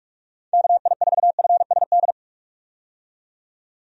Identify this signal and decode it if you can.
Morse code, heard as a keyed tone, about 45 wpm, 700 Hz. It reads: KI4FSD
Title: USA Call Signs - 6 Characters - 45wpm